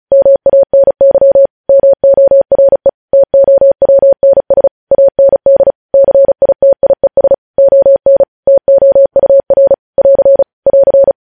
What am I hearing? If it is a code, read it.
Morse code, heard as a keyed tone, about 36 wpm, 550 Hz. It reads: MANY MORE TOWNS AND CITIES ON TOUR <AR> <AR>